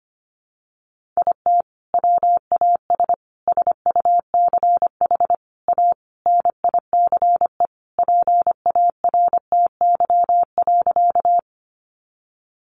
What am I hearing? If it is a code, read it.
Morse code, heard as a keyed tone, about 25 wpm, 700 Hz. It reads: IT WAS SUCH A NICE PARTY.
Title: Sentences from Top 1400 Words 25wpm